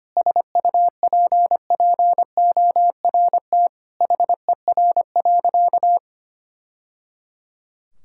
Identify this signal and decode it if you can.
Morse code, heard as a keyed tone, about 25 wpm, 700 Hz. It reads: SUPPORT HER.
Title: Sentences from Top 1000 Words - Review Part 5 - 25wpm